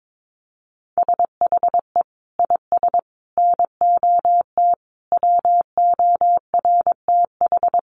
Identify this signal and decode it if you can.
Morse code, heard as a keyed tone, about 22 wpm, 700 Hz. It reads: SHE IS NOT WORTH
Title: Sentences from Top 1100 Words 22wpm